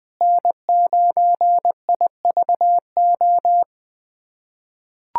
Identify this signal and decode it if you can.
Morse code, heard as a keyed tone, about 20 wpm, 700 Hz. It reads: N9IVO
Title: USA Callsign DE Callsign - 5 Characters - 20wpm